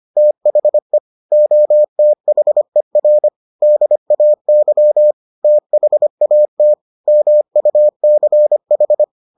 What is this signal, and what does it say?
Morse code, heard as a keyed tone, about 25 wpm, 600 Hz. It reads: THE OTHER DAY THAT MUCH